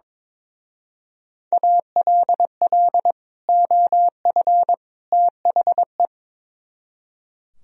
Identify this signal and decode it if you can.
Morse code, heard as a keyed tone, about 24 wpm, 700 Hz. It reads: ALL OF THE
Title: Speed-Racing - Sets of 3 Words 22wpm